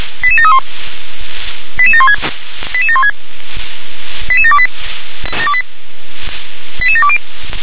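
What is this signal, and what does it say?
Five-tone selective call (ZVEI), heard as quick runs of five tones, tones of about 70 ms. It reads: 88731 88716 88716 88728 88720
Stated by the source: Five-tone ZVEI selective call received on SDR.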